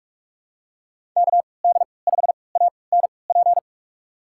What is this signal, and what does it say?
Morse code, heard as a keyed tone, about 46 wpm, spaced as for 25 wpm, 700 Hz. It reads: KD5ANP
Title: USA Call Signs - 6 Characters - 3x Character Spacing - 45wpm